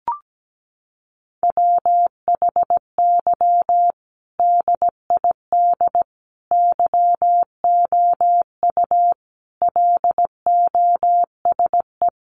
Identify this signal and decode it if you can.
Morse code, heard as a keyed tone, about 17 wpm, 700 Hz. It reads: WHY DID YOU LOSE
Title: Sentences from Top 1100 Words 17wpm